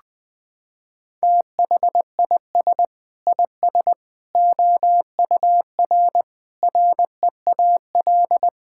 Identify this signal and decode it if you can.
Morse code, heard as a keyed tone, about 21 wpm, 700 Hz. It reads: THIS IS OUR REAL
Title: Sentences from Top 1000 Words - Review Part 5 - 20wpm